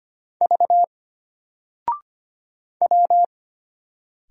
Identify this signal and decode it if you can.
Morse code, heard as a keyed tone, about 25 wpm, 700 Hz. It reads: V W
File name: Single-Letter-Number-25wpm